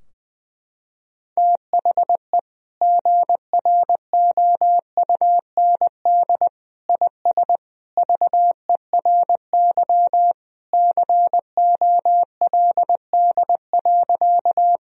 Morse code, heard as a keyed tone, about 20 wpm, 700 Hz. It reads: THE GROUND IS VERY COLD.
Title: Sentences from Top 1000 Words - Review Part 2 - 20wpm